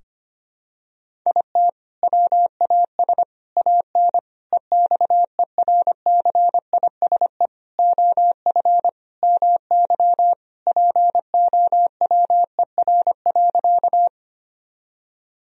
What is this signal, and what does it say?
Morse code, heard as a keyed tone, about 25 wpm, 700 Hz. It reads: IT WAS AN EXERCISE OF MY POWER.
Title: Sentences from Top 1000 Words - Review Part 4 - 25wpm